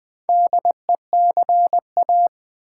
Morse code, heard as a keyed tone, about 20 wpm, 690 Hz. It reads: DECA
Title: Build A Word - 2 Letters At A Time - 6-Letter Words - 20wpm